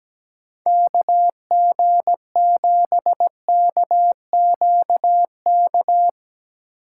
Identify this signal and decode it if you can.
Morse code, heard as a keyed tone, about 17 wpm, 700 Hz. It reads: KG7KQK
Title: USA Call Signs - 6 Characters - 2x Character Spacing - 17wpm